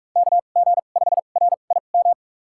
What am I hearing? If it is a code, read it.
Morse code, heard as a keyed tone, about 46 wpm, spaced as for 33 wpm, 700 Hz. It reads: KG5RIM
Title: USA Call Signs - 6 Characters - 2x Character Spacing - 45wpm